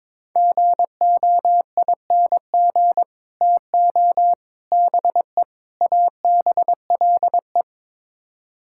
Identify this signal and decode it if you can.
Morse code, heard as a keyed tone, about 22 wpm, 700 Hz. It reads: GOING TO BE ABLE
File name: practice / 22wpm / Build A Phrase - 5-Word Phrases - 22wpm